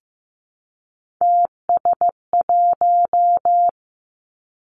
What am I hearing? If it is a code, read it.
Morse code, heard as a keyed tone, about 15 wpm, 700 Hz. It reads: TS1
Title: Three Challenging Letter-Number Combinations 15wpm